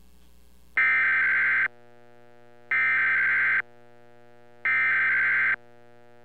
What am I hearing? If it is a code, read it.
Emergency Alert System (SAME) tones (header bursts), heard as harsh buzzing bursts. It is heard from 0.8 to 5.5 s.